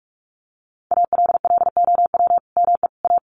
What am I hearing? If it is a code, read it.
Morse code, heard as a keyed tone, about 44 wpm, 700 Hz. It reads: ALLOW ME A
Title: Sentences from Top 1000 Words 45wpm